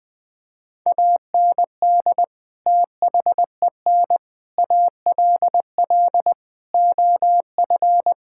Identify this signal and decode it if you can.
Morse code, heard as a keyed tone, about 20 wpm, 700 Hz. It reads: AND THEN ALL OF
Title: Sentences with 6-Word Phrases - 20wpm